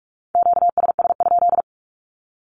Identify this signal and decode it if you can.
Morse code, heard as a keyed tone, about 44 wpm, 700 Hz. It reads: QSS?
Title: Q-Codes Repeated 3x 45wpm